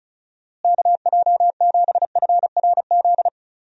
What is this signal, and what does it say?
Morse code, heard as a keyed tone, about 36 wpm, 700 Hz. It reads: KJ7FRZ
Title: USA Call Signs - 6 Characters - Repeated 3x - 35wpm